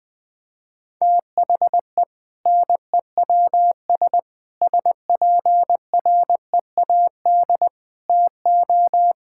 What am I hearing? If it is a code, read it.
Morse code, heard as a keyed tone, about 21 wpm, 700 Hz. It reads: THE NEWS SPREAD TO
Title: Sentences with 5-Word Phrases - Spoken Beginning - 20wpm